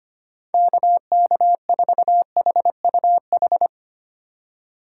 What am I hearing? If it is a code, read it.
Morse code, heard as a keyed tone, about 25 wpm, 700 Hz. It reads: KK4HUH
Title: USA Call Signs - 2x Character Spacing - 25wpm